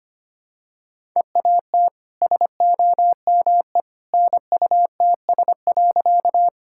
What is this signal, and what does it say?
Morse code, heard as a keyed tone, about 25 wpm, 700 Hz. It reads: EAT SOME NUTS.
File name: Easy Sentences with Hints - 25wpm